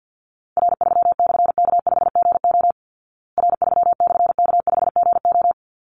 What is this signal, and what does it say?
Morse code, heard as a keyed tone, about 50 wpm, 700 Hz. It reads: R3/K5GO R3/K5GO
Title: Call Signs Repeated 3x - 50wpm